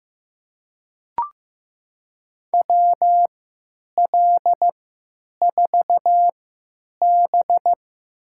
Morse code, heard as a keyed tone, about 15 wpm, spaced as for 9 wpm, 700 Hz. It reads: WL4B
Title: USA Call Signs - 3x Character Spacing - 15wpm